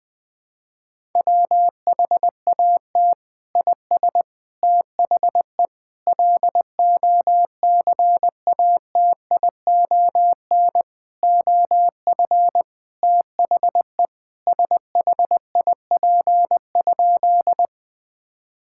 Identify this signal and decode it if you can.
Morse code, heard as a keyed tone, about 20 wpm, 700 Hz. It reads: WHAT IS THE LOCATION OF THE SHIP?
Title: Sentences from Top 1000 Words - Review Part 6 - 20wpm